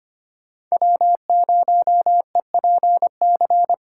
Morse code, heard as a keyed tone, about 25 wpm, 700 Hz. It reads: W0EPC
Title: K1USN SST QSO Practice - 25wpm